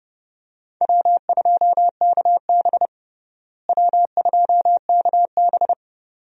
Morse code, heard as a keyed tone, about 32 wpm, 700 Hz. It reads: W2KB W2KB
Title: USA Call Signs - 4 Characters - Repeated 2x - 30wpm